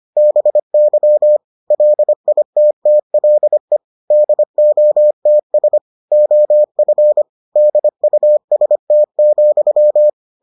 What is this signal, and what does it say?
Morse code, heard as a keyed tone, about 25 wpm, 600 Hz. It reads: BY LITTLE DOTS OF DUST,